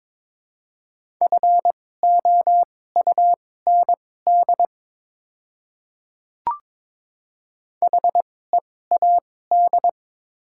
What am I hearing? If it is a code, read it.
Morse code, heard as a keyed tone, about 22 wpm, spaced as for 16 wpm, 700 Hz. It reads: FOUND HEAD
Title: 2x Character Spacing - Top 300 Words - 22wpm